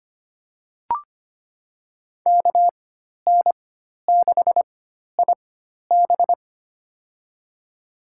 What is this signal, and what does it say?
Morse code, heard as a keyed tone, about 25 wpm, spaced as for 12 wpm, 700 Hz. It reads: KN6IB
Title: USA Call Signs - 5 Characters - 4x Character Spacing - 25wpm